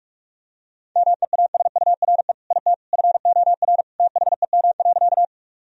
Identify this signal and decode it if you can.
Morse code, heard as a keyed tone, about 42 wpm, 700 Hz. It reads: MEASURE IT FOR THEM.
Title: Sentences from Top 1000 Words - Review Part 2 - 45wpm